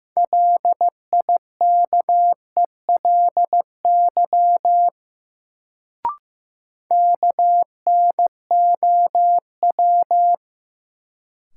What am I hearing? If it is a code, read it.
Morse code, heard as a keyed tone, about 15 wpm, 700 Hz. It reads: LIKELY KNOW
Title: Sets of 2 Words - New Word Drill - 15wpm